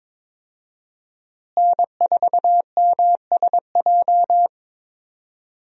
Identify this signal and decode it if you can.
Morse code, heard as a keyed tone, about 22 wpm, 700 Hz. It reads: N4MSJ
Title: USA Call Signs - 5 Characters - Repeated 2x - 22wpm